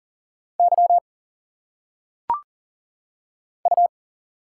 Morse code, heard as a keyed tone, about 38 wpm, 700 Hz. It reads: Y U
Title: Single Letters - Phonetic Pronunciation 40wpm